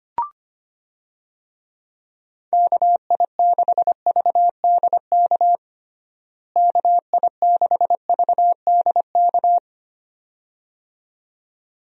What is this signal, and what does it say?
Morse code, heard as a keyed tone, about 25 wpm, 700 Hz. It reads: KI6VDK KI6VDK
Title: USA Call Signs - 6 Characters - Repeated 2x - 25wpm